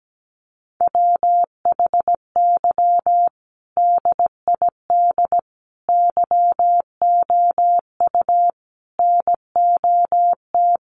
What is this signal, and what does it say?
Morse code, heard as a keyed tone, about 17 wpm, 700 Hz. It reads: WHY DID YOU NOT
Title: Sentences from Top 1300 Words 17wpm